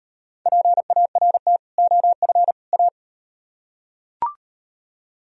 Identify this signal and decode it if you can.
Morse code, heard as a keyed tone, about 35 wpm, 700 Hz. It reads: PART OF A